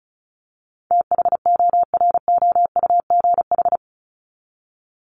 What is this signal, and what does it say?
Morse code, heard as a keyed tone, about 34 wpm, 700 Hz. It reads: THOROUGH
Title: LY Suffix - Top 100 Words with Base Words 35wpm